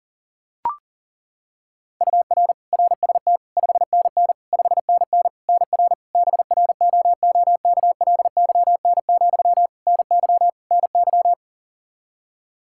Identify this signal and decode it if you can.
Morse code, heard as a keyed tone, about 37 wpm, 700 Hz. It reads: UR RST 5NN 5NN NR BROOKLYN, NY NY
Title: K1USN SST QSO Practice 40wpm